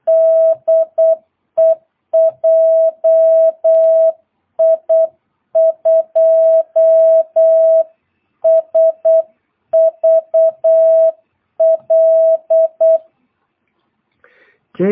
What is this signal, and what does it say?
Morse code, heard as a keyed tone, about 8 wpm, 650 Hz. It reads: DE JI2SVL
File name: jh8cbh / 2022cw